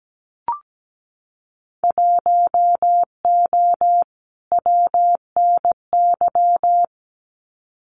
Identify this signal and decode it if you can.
Morse code, heard as a keyed tone, about 17 wpm, 700 Hz. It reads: 1O WNY
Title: Winter Field Day Exchanges Encore 17wpm